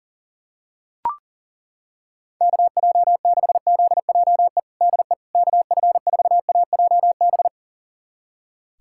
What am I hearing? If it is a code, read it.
Morse code, heard as a keyed tone, about 40 wpm, 700 Hz. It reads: KJ6ZJE DE KF4AJB
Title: USA Callsign DE Callsign 40wpm